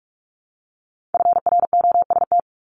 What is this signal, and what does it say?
Morse code, heard as a keyed tone, about 46 wpm, 700 Hz. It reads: FROST
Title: Multi-letter Phonemes 45wpm